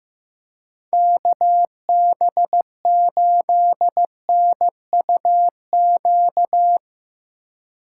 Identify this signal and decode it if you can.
Morse code, heard as a keyed tone, about 16 wpm, 700 Hz. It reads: KB8NUQ
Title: USA Call Signs - 6 Characters - 4x Character Spacing - 15wpm